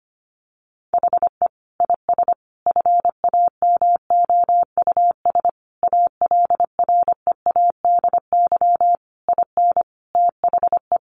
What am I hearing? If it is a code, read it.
Morse code, heard as a keyed tone, about 25 wpm, 700 Hz. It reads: HE IS FAMOUS ALREADY IN THE
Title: Sentences from Top 1500 Words 25wpm